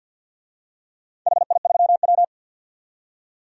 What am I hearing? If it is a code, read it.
Morse code, heard as a keyed tone, about 52 wpm, 700 Hz. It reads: HI3W
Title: DXCC-100 Call Signs - 4 Characters - 50wpm